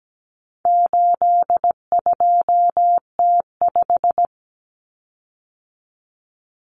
Morse code, heard as a keyed tone, about 17 wpm, 700 Hz. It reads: 82T5
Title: Common Cut Numbers - 4-Digit 17wpm